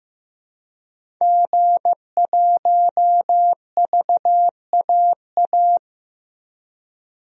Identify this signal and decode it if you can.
Morse code, heard as a keyed tone, about 16 wpm, 700 Hz. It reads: G1VAA
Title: DXCC-100 Call Signs - 5 Characters - 15wpm